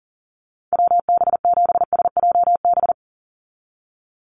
Morse code, heard as a keyed tone, about 37 wpm, 700 Hz. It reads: WB7SJB
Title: USA Call Signs 40wpm